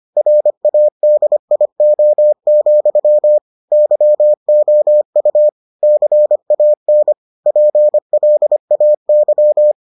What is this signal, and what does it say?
Morse code, heard as a keyed tone, about 25 wpm, 600 Hz. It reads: RADIO, YOU CAN PLAY